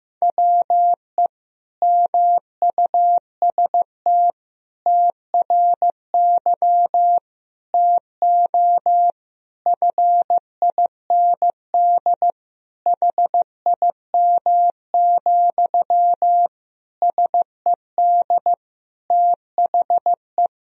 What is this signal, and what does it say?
Morse code, heard as a keyed tone, about 16 wpm, 700 Hz. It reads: WE MUST TRY TO FIND HIM, SED THE
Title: Sentences from Top 400 Words and Abbreviations 15wpm